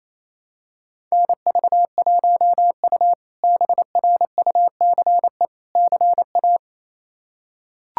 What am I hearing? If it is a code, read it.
Morse code, heard as a keyed tone, about 28 wpm, 700 Hz. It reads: NV1U BRUCE CA
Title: K1USN SST QSO Practice - 28wpm